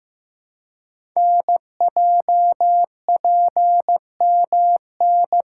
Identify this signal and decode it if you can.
Morse code, heard as a keyed tone, about 15 wpm, 690 Hz. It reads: NJPMN
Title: Code Groups - 5 Letters - 15wpm